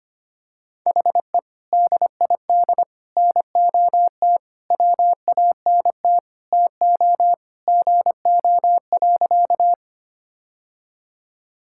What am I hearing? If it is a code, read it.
Morse code, heard as a keyed tone, about 25 wpm, 700 Hz. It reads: HE DID NOT WANT TO GO.
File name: practice / 25wpm / Sentences with 3-Word Phrases - 25wpm